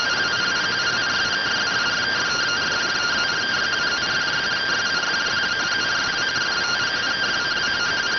Two-tone teleprinter signal, about 170 Hz shift.